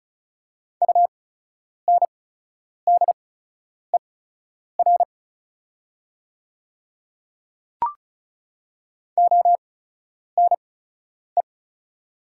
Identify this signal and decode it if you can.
Morse code, heard as a keyed tone, about 36 wpm, spaced as for 10 wpm, 700 Hz. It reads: UNDER ONE
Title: 8x Character Spacing - Top 300 Words - 35wpm